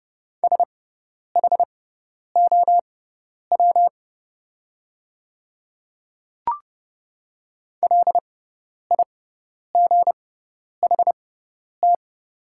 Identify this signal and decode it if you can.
Morse code, heard as a keyed tone, about 30 wpm, spaced as for 10 wpm, 700 Hz. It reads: SHOW LIGHT
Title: 6x Character Spacing - Top 300 Words - 30wpm